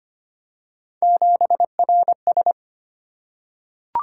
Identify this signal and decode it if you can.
Morse code, heard as a keyed tone, about 25 wpm, 700 Hz. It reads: 7RS
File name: Three Letter-Number Combinations - 25wpm